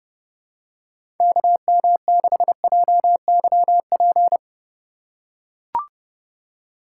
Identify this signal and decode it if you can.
Morse code, heard as a keyed tone, about 31 wpm, 700 Hz. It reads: KM6JYP
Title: USA Call Signs - 6 Characters - 3x Character Spacing - 30wpm